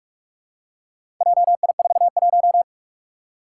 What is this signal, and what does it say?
Morse code, heard as a keyed tone, about 46 wpm, 700 Hz. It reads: JI41
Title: Four Challenging Letter-Number Combinations 45wpm